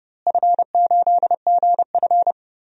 Morse code, heard as a keyed tone, about 30 wpm, 700 Hz. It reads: F8GF
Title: DXCC-100 Call Signs - 4 Characters - Repeated 2x - 30wpm